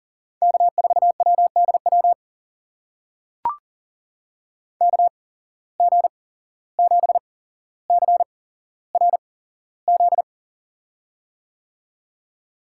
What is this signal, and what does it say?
Morse code, heard as a keyed tone, about 39 wpm, spaced as for 16 wpm, 700 Hz. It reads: K4WDW KG7CRZ
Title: USA Call Signs - 8x Character Spacing - 40wpm